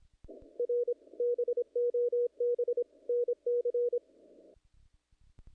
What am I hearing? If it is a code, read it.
Morse code, heard as a keyed tone, about 25 wpm, 470 Hz. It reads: R BOB NC
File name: contests / naqp-cw-k5tr-op-k5pi- -1800